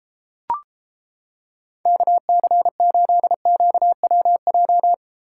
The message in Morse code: KC8QWJ

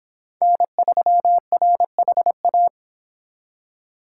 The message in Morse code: N3RHA

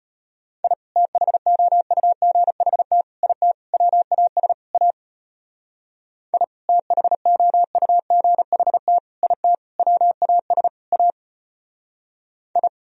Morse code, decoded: I THOUGHT IT WAS A I THOUGHT IT WAS A I

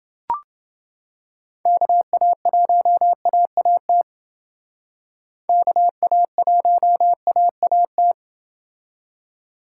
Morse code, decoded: KA1AAT KA1AAT

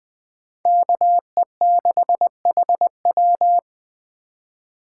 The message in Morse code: KE6HW

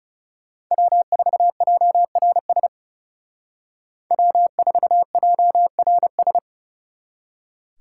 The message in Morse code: W4JRS W4JRS

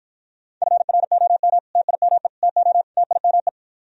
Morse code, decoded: FROM TIME TO TIME

